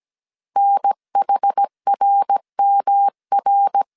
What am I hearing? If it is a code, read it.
Morse code, heard as a keyed tone, about 17 wpm, 800 Hz. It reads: NHRMR
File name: CW / DET / 1 3 1500C S A 16wpm 2141